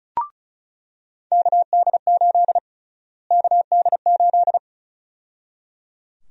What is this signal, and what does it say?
Morse code, heard as a keyed tone, about 34 wpm, 700 Hz. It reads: KD8 KD8
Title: USA Call Sign Prefixes - Repeated 2x - 35wpm